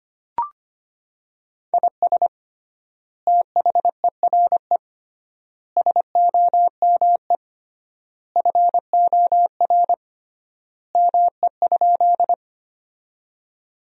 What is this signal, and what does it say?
Morse code, heard as a keyed tone, about 25 wpm, 700 Hz. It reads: IS THERE SOME FOR ME?
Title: 3x Word-Spacing - Sentences from Top 200 Words - 25wpm